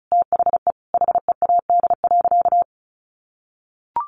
Morse code, heard as a keyed tone, about 34 wpm, 700 Hz. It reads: THE HEAD.
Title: Sentences from Top 1200 Words 35wpm